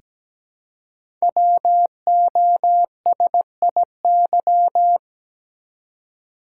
Morse code, heard as a keyed tone, about 17 wpm, 700 Hz. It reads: WOSIY
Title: Code Groups - 5 Letter:Numbers - 17wpm